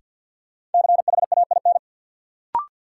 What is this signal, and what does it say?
Morse code, heard as a keyed tone, about 53 wpm, 700 Hz. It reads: CHAIN
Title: Sentences from Top 1500 Words - New Word Drill - 50wpm